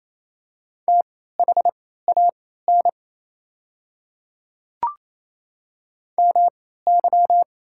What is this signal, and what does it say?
Morse code, heard as a keyed tone, about 28 wpm, spaced as for 16 wpm, 700 Hz. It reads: THAN MY
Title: 3x Character Spacing - Top 100 Words - 28wpm